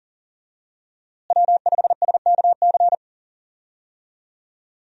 Morse code, heard as a keyed tone, about 39 wpm, 700 Hz. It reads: W5SKC
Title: USA Call Signs - 5 Characters - Repeated 2x - 40wpm